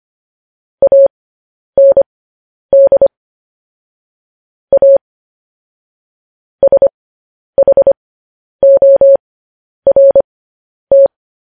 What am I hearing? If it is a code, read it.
Morse code, heard as a keyed tone, about 25 wpm, spaced as for 10 wpm, 550 Hz. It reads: AND A SHORT
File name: podcast / News headlines-1x-25x10